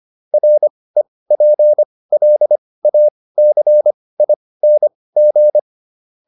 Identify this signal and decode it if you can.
Morse code, heard as a keyed tone, about 25 wpm, spaced as for 18 wpm, 600 Hz. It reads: REPLACING